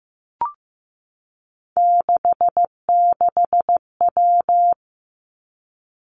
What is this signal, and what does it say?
Morse code, heard as a keyed tone, about 15 wpm, 700 Hz. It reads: 66W
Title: Three Challenging Letter-Number Combinations 15wpm